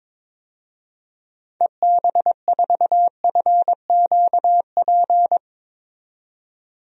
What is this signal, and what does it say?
Morse code, heard as a keyed tone, about 22 wpm, 700 Hz. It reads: EB4FQP
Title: DXCC-100 Call Signs - 6 Characters - 22wpm